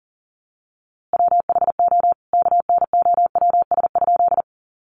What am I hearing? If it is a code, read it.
Morse code, heard as a keyed tone, about 40 wpm, 700 Hz. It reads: WHO KNOWS?
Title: A Sound Of Thunder 40wpm